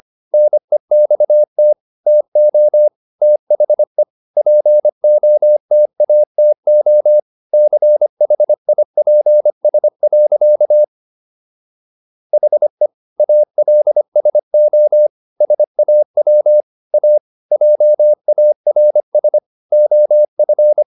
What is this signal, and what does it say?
Morse code, heard as a keyed tone, about 25 wpm, 600 Hz. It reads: NEXT TO THE POTATO CHIPS. HE ALSO SAW A JARS OF